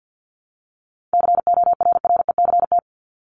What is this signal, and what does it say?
Morse code, heard as a keyed tone, about 50 wpm, 700 Hz. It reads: CORRECT
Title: Top 500 Words Repeated 50wpm